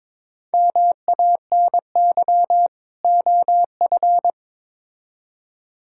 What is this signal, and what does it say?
Morse code, heard as a keyed tone, about 22 wpm, 700 Hz. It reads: MANY OF
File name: Build A Phrase - 2-Word Phrases - 22wpm